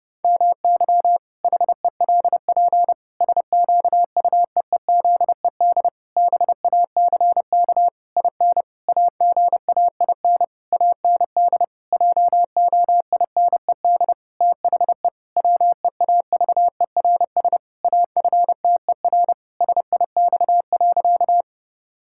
Morse code, heard as a keyed tone, about 32 wpm, 700 Hz. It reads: MY HELP SQUEEZED BACK IN AGAIN AND JOINED THE WEAVERS AFTER SIX.
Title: Pangram Challenge v2 - 30wpm